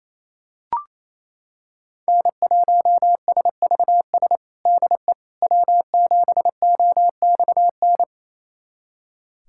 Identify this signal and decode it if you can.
Morse code, heard as a keyed tone, about 28 wpm, 700 Hz. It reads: N1SVS DE W7OXN
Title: USA Callsign DE Callsign - 5 Characters - 28wpm